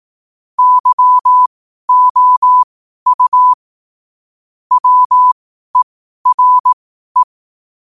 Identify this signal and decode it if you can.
Morse code, heard as a keyed tone, about 18 wpm, spaced as for 12 wpm, 1000 Hz. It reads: YOU WERE
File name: qotdfiles / 12wpm